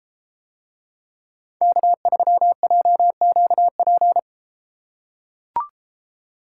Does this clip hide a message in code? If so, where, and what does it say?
Yes: Morse code from 1.6 to 4.2 s, heard as a keyed tone, about 33 wpm, 700 Hz. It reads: K3JQP